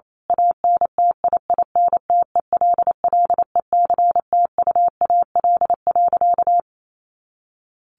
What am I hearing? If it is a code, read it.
Morse code, heard as a keyed tone, about 28 wpm, 700 Hz. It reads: ANTIINTELLECTUAL.
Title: A Sound Of Thunder 28wpm